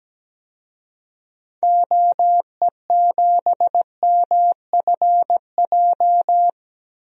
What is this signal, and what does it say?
Morse code, heard as a keyed tone, about 17 wpm, 700 Hz. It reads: OE7MFJ
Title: DXCC-100 Call Signs - 6 Characters - 17wpm